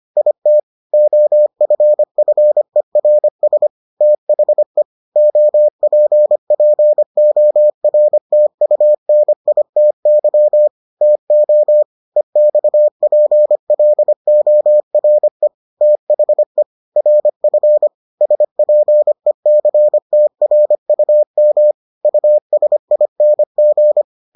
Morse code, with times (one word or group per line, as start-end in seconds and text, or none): 0.2-0.6 IT
0.9-3.7 OFFERS
4.0-4.8 THE
5.2-10.7 OPPORTUNITY
11.0-11.8 TO
12.2-15.5 EXPLORE
15.8-16.6 THE
17.0-17.9 RF
18.2-21.7 SPECTRUM
22.0-24.0 USING